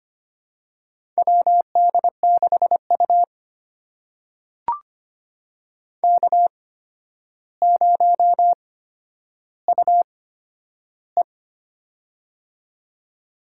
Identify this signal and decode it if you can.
Morse code, heard as a keyed tone, about 25 wpm, spaced as for 11 wpm, 700 Hz. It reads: WD6U K0UE
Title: USA Call Signs - 4 Characters - 8x Character Spacing - 25wpm